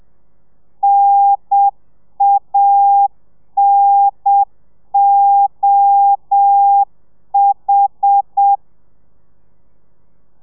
Morse code, heard as a keyed tone, about 7 wpm, 800 Hz. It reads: NANOH